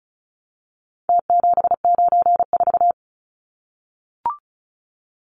Morse code, heard as a keyed tone, about 34 wpm, 700 Hz. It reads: T794
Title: Numbers - Contest Style - 4-Digit 35wpm